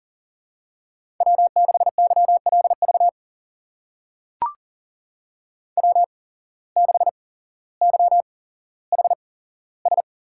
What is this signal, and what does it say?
Morse code, heard as a keyed tone, about 41 wpm, spaced as for 17 wpm, 700 Hz. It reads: W6YLV W6YHS
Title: USA Call Signs - 5 Characters - 8x Character Spacing - 40wpm